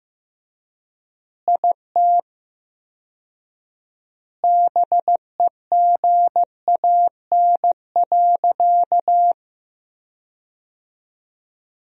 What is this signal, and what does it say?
Morse code, heard as a keyed tone, about 15 wpm, 700 Hz. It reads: IT BEGAN.
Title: 4x Word-Spacing - Sentences from Top 300 Words - 15wpm